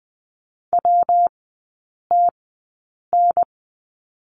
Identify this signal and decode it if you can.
Morse code, heard as a keyed tone, about 20 wpm, spaced as for 8 wpm, 700 Hz. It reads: WTN
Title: 133 Lesson 22e - Question Mark - Three Characters 20wpm